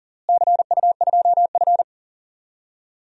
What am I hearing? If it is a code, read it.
Morse code, heard as a keyed tone, about 38 wpm, 700 Hz. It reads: CU2F
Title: DXCC-100 Call Signs - 4 Characters - Repeated 3x - 40wpm